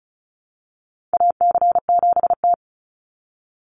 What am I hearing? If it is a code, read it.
Morse code, heard as a keyed tone, about 34 wpm, 700 Hz. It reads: AC7T
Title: Call Signs - 35wpm